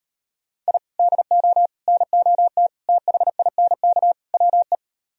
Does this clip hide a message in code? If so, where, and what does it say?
Yes: Morse code from 0.7 to 4.8 s, heard as a keyed tone, about 38 wpm, 700 Hz. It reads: I DO NOT THINK WE